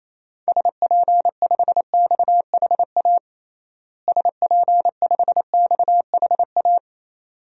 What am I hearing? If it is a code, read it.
Morse code, heard as a keyed tone, about 28 wpm, 700 Hz. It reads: SP5XHA SP5XHA
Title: DXCC-100 Call Signs - 6 Characters - Repeated 2x - 28wpm